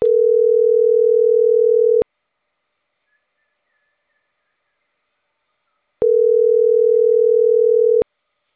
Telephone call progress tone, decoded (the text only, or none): ringback tone